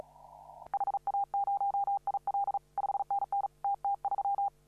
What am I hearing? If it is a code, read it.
Morse code, heard as a keyed tone, about 35 wpm, 800 Hz. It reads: HA0IL 5NN TT3
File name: CW / RHOB CW 2017 HA6P 0001